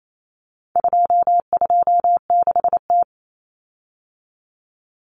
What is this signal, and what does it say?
Morse code, heard as a keyed tone, about 28 wpm, 700 Hz. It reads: 226T
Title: Common Cut Numbers - 4-Digit 28wpm